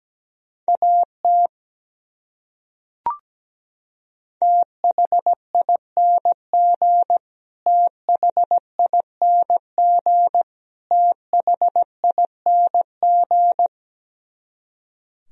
Morse code, heard as a keyed tone, about 17 wpm, 700 Hz. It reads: AT THING THING THING
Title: Top 100 Words Repeated 17wpm